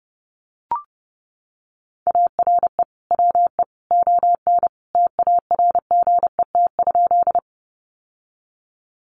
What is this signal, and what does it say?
Morse code, heard as a keyed tone, about 30 wpm, 700 Hz. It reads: ARE WE ON TARGET?
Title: Sentences from Top 1200 Words 30wpm